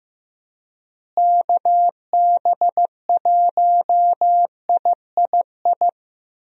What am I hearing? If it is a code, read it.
Morse code, heard as a keyed tone, about 15 wpm, 700 Hz. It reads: KB1III
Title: USA Call Signs - 6 Characters - Repeated 3x - 15wpm